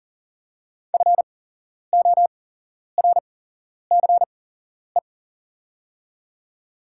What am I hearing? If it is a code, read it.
Morse code, heard as a keyed tone, about 40 wpm, spaced as for 11 wpm, 700 Hz. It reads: FORCE
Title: 8x Character Spacing - Top 500 Words - 40wpm